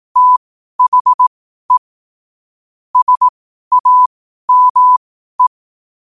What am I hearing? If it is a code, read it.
Morse code, heard as a keyed tone, about 18 wpm, spaced as for 12 wpm, 1000 Hz. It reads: THE SAME
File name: qotdfiles / 12wpm